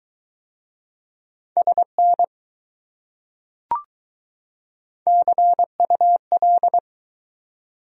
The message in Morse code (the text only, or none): SN CUL